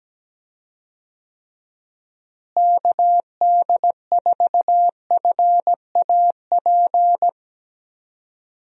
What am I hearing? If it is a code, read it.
Morse code, heard as a keyed tone, about 17 wpm, 700 Hz. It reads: KD4FAP
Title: USA Call Signs - 6 Characters - Repeated 2x - 17wpm